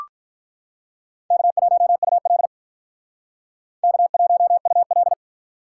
Morse code, heard as a keyed tone, about 45 wpm, 700 Hz. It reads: K1UL K1UL